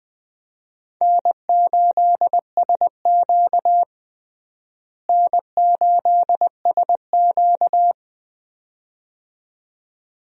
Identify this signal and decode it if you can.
Morse code, heard as a keyed tone, about 20 wpm, 700 Hz. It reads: N8SQ N8SQ
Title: USA Call Signs - 4 Characters - Repeated 2x - 20wpm